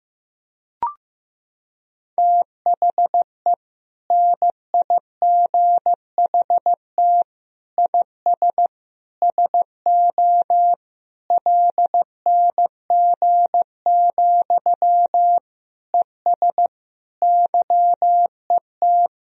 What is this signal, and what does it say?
Morse code, heard as a keyed tone, about 15 wpm, 700 Hz. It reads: THE NIGHT IS SO LNG, ES YET
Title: Sentences from Top 500 Words and Abbreviations 15wpm